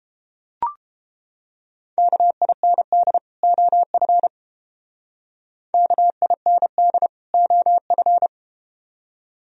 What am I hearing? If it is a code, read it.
Morse code, heard as a keyed tone, about 30 wpm, 700 Hz. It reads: KIND OF KIND OF